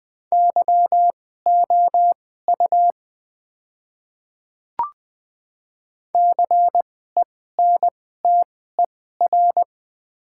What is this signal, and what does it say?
Morse code, heard as a keyed tone, about 20 wpm, spaced as for 14 wpm, 690 Hz. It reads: YOU CENTER
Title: 2x Character Spacing - Top 500 Words - 20wpm